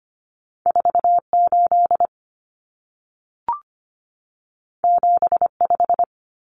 Morse code, heard as a keyed tone, about 25 wpm, 700 Hz. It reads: 48 75
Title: Common Cut Numbers - 2-Digit 25wpm